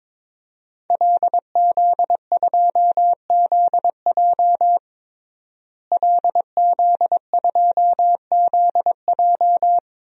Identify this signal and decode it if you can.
Morse code, heard as a keyed tone, about 22 wpm, 700 Hz. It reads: LZ2ZJ LZ2ZJ
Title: DXCC-100 Call Signs - 5 Characters - Repeated 2x - 22wpm